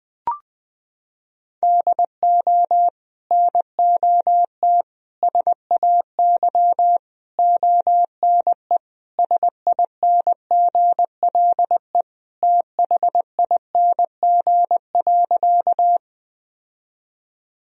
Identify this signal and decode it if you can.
Morse code, heard as a keyed tone, about 21 wpm, 700 Hz. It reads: DO NOT SAY ONE SINGLE THING.
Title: Sentences from Top 1000 Words - Review Part 4 - 20wpm